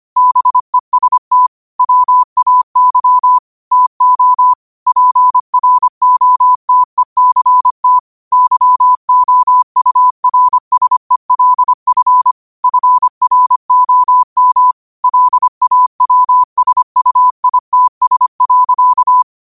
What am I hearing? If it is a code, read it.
Morse code, heard as a keyed tone, about 25 wpm, 1000 Hz. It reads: BEST WAY TO PROTECT YOURSELF FROM LAWSUITS.
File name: qotdfiles / 25wpm